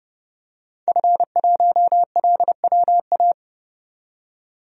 Morse code, heard as a keyed tone, about 28 wpm, 690 Hz. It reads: F1LWA
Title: DXCC-100 Call Signs - 5 Characters - 30wpm